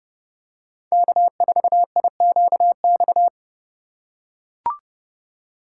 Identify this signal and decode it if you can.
Morse code, heard as a keyed tone, about 31 wpm, 700 Hz. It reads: K4IQX
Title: USA Call Signs - 5 Characters - 4x Character Spacing - 30wpm